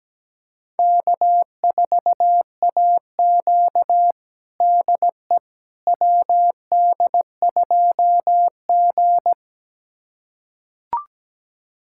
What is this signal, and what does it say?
Morse code, heard as a keyed tone, about 17 wpm, 700 Hz. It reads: K4AQ DE WD2G
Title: USA Callsign DE Callsign - 4 Characters - 17wpm